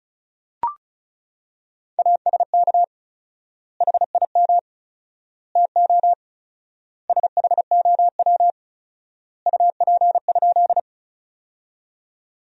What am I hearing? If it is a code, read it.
Morse code, heard as a keyed tone, about 36 wpm, 700 Hz. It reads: ASK HIM TO SHOW UP?
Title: 4x Word-Spacing - Sentences from Top 200 Words - 35wpm